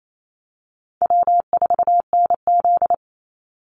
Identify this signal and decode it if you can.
Morse code, heard as a keyed tone, about 28 wpm, 700 Hz. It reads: W4NZ
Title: Easy Call Signs - 28wpm